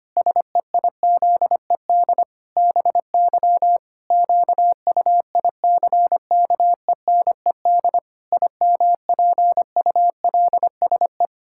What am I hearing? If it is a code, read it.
Morse code, heard as a keyed tone, about 25 wpm, 700 Hz. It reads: SEIZED BY QUICKENED IMPULSE
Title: Pangram Challenge v2 - 25wpm